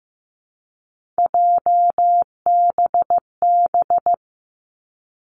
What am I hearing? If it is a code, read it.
Morse code, heard as a keyed tone, about 15 wpm, 700 Hz. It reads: JBB
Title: Three Challenging Letter-Number Combinations 15wpm